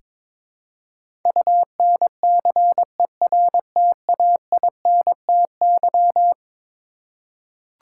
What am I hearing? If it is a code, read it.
Morse code, heard as a keyed tone, about 21 wpm, 700 Hz. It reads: UNCERTAINTY
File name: practice / 22wpm / Eleven Letter Words - 22wpm